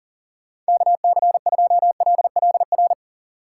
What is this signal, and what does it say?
Morse code, heard as a keyed tone, about 40 wpm, 700 Hz. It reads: KC2LLR
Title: USA Call Signs - 6 Characters - 3x Character Spacing - 40wpm